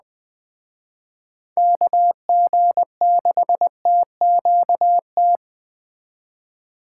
Morse code, heard as a keyed tone, about 21 wpm, 700 Hz. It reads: KG6TQT
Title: Speed Racing - USA 2x3 Call Signs 17wpm